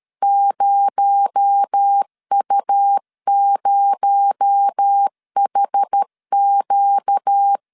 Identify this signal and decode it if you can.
Morse code, heard as a keyed tone, about 13 wpm, 800 Hz. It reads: 0U0HQ
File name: LEZIONI / 6 PKX0 / 6 PKX0 12wpm 2231